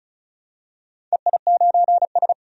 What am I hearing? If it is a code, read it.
Morse code, heard as a keyed tone, about 36 wpm, 700 Hz. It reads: EI9S
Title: DXCC-100 Call Signs - 4 Characters - 35wpm